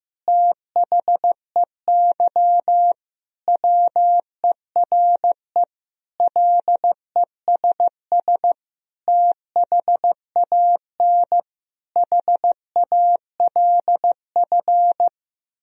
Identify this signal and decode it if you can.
Morse code, heard as a keyed tone, about 16 wpm, 700 Hz. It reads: THEY WERE LESS THAN HALF
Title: Sentences from Top 1000 Words - Review Part 2 - 15wpm